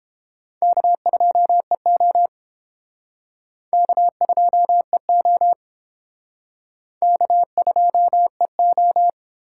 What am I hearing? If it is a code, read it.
Morse code, heard as a keyed tone, about 29 wpm, 700 Hz. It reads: K2EO K2EO K2EO